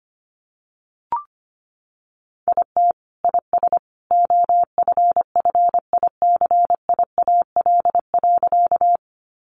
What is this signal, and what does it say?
Morse code, heard as a keyed tone, about 25 wpm, 700 Hz. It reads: IT IS OFFICIAL.
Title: Sentences from Top 1000 Words 25wpm